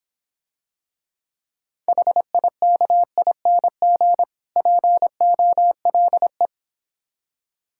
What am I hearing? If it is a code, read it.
Morse code, heard as a keyed tone, about 26 wpm, 700 Hz. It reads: HIKING POLE